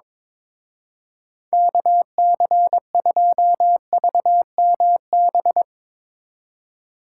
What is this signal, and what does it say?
Morse code, heard as a keyed tone, about 22 wpm, 700 Hz. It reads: KC2VMB
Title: Speed Racing - USA 2x3 Call Signs 22wpm